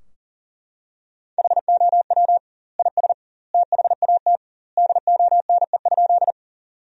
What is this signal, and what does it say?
Morse code, heard as a keyed tone, about 37 wpm, 700 Hz. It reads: HOW IS THAT DONE?
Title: 2x Word-Spacing - Sentences from Top 500 Words - 40wpm